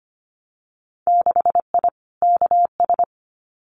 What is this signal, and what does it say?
Morse code, heard as a keyed tone, about 25 wpm, 700 Hz. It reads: 6I KS
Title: Winter Field Day Exchanges 25wpm